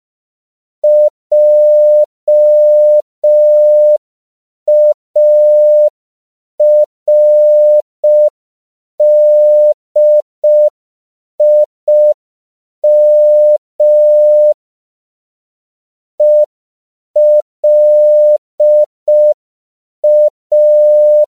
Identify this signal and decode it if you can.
Morse code, heard as a keyed tone, about 5 wpm, 600 Hz. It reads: JARDIM ELA
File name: 5PPM